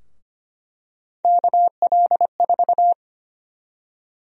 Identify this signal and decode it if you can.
Morse code, heard as a keyed tone, about 25 wpm, 700 Hz. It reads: KL4
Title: USA Call Sign Prefixes - Repeated 2x - 25wpm